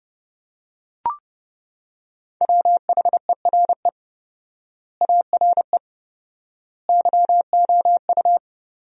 Morse code, heard as a keyed tone, about 30 wpm, 700 Hz. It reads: WHERE ARE YOU
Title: 4x Word-Spacing - Sentences from Top 200 Words - 30wpm